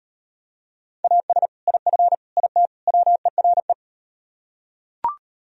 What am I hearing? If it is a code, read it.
Morse code, heard as a keyed tone, about 38 wpm, 700 Hz. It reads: AS IF IT WERE